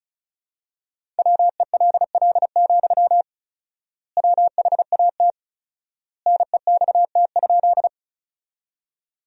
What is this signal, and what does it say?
Morse code, heard as a keyed tone, about 36 wpm, 700 Hz. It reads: WELL, WHAT NEXT?
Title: 4x Word-Spacing - Sentences from Top 300 Words - 35wpm